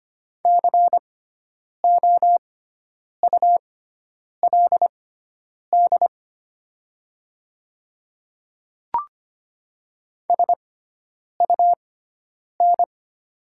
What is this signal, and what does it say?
Morse code, heard as a keyed tone, about 25 wpm, spaced as for 9 wpm, 700 Hz. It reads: COULD SUN
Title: 6x Character Spacing - Top 300 Words - 25wpm